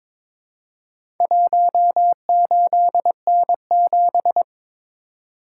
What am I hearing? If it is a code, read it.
Morse code, heard as a keyed tone, about 23 wpm, 700 Hz. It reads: 18N7
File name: practice / 22wpm / Four Letter-Number Combinations - 22wpm